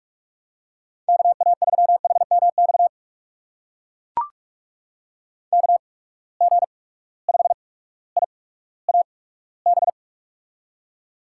Morse code, heard as a keyed tone, about 44 wpm, spaced as for 19 wpm, 700 Hz. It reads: KA3HMX KG5IAB
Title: USA Call Signs - 6 Characters - 8x Character Spacing - 45wpm